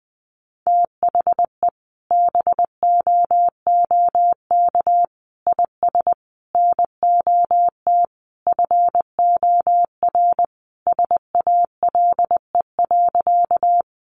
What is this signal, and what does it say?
Morse code, heard as a keyed tone, about 20 wpm, 700 Hz. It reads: THE BOOK IS NOT FOR SALE.
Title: Sentences from Top 1400 Words 20wpm